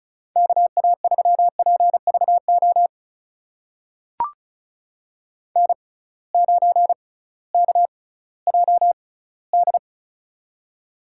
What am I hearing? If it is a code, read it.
Morse code, heard as a keyed tone, about 36 wpm, spaced as for 19 wpm, 700 Hz. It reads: KA3PVO N9KJD
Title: USA Call Signs - 6x Character Spacing - 35wpm